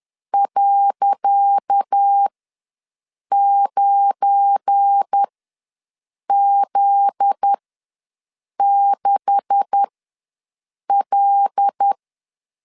Morse code, heard as a keyed tone, about 10 wpm, spaced as for 6 wpm, 800 Hz. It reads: .9Z6L
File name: LEZIONI / 10 SIGNS / 10C DOT ws 10wpm 4213